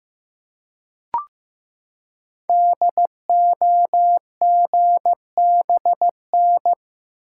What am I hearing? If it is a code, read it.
Morse code, heard as a keyed tone, about 15 wpm, 700 Hz. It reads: DOGBN
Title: Code Groups - 5 Letters - 15wpm